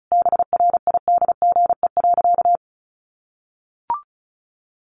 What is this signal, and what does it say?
Morse code, heard as a keyed tone, about 34 wpm, 700 Hz. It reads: BRIDGE.
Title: Sentences from Top 1400 Words 35wpm